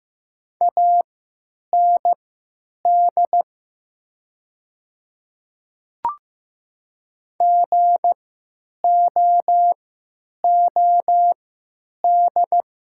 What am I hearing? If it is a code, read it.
Morse code, heard as a keyed tone, about 15 wpm, spaced as for 9 wpm, 700 Hz. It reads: AND GOOD
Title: 3x Character Spacing - Top 300 Words - 15wpm